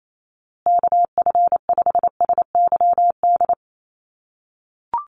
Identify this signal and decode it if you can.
Morse code, heard as a keyed tone, about 28 wpm, 700 Hz. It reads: KF5SYD
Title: USA Call Signs Repeated 2x 28wpm